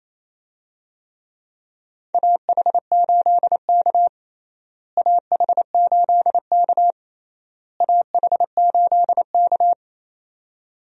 Morse code, heard as a keyed tone, about 28 wpm, 700 Hz. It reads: AH8K AH8K AH8K
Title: USA Call Signs - 4 Characters - Repeated 3x - 28wpm